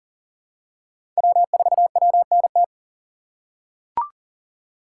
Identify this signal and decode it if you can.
Morse code, heard as a keyed tone, about 40 wpm, 700 Hz. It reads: W4WNT
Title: K1USN SST QSO Practice - 40wpm